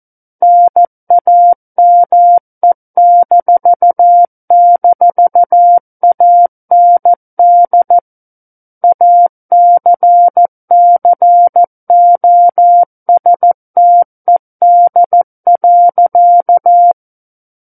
Morse code, heard as a keyed tone, about 14 wpm, 700 Hz. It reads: NAME--AND ACCOSTED.
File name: Book One / 05-Chapter 5 - 14 WPM